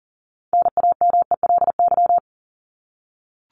Morse code, heard as a keyed tone, about 40 wpm, 700 Hz. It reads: NAMELY
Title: LY Suffix - Top 100 Words with Base Words 40wpm